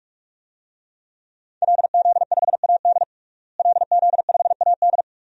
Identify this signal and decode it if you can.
Morse code, heard as a keyed tone, about 46 wpm, 700 Hz. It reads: LZ5AD LZ5AD
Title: DXCC-100 Call Signs - 5 Characters - Repeated 2x - 45wpm